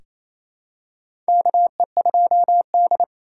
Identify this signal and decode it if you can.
Morse code, heard as a keyed tone, about 28 wpm, 700 Hz. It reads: KE2D
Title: K1USN SST QSO Practice - 28wpm